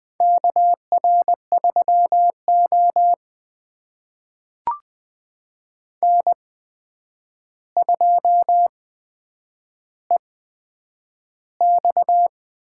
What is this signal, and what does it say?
Morse code, heard as a keyed tone, about 20 wpm, spaced as for 9 wpm, 700 Hz. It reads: KR3O N2EX
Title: USA Call Signs - 4 Characters - 8x Character Spacing - 20wpm